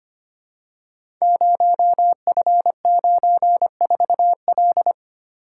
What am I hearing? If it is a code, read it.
Morse code, heard as a keyed tone, about 25 wpm, 700 Hz. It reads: 0F94L
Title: Code Groups - 5 Letter:Numbers - 25wpm